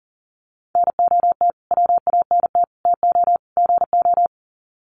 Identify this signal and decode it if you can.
Morse code, heard as a keyed tone, about 40 wpm, 700 Hz. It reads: NOT WANT TO GO
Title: Sets of 4 Words 40wpm